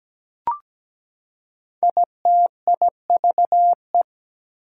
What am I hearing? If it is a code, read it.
Morse code, heard as a keyed tone, about 17 wpm, 700 Hz. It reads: ITIVE
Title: Prefixes and Suffixes 17wpm